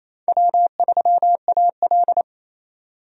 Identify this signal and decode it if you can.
Morse code, heard as a keyed tone, about 28 wpm, 700 Hz. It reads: W3AL
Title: USA Call Signs - 4 Characters - 28wpm